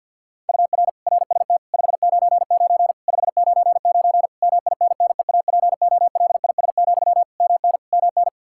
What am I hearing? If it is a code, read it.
Morse code, heard as a keyed tone, about 47 wpm, 700 Hz. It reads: UR RST 599 599 MINNEAPOLIS, MN MN
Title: Practice QSO - Exchange 1 - Beginning - 50wpm